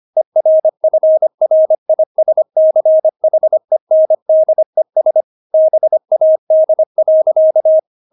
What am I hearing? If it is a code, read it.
Morse code, heard as a keyed tone, about 25 wpm, 630 Hz. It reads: ERFRISCHENDES BAD.
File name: Witz6 25wpm